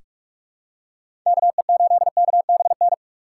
Morse code, heard as a keyed tone, about 46 wpm, 700 Hz. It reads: KE8KBN
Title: USA Call Signs - 4x Character Spacing - 45wpm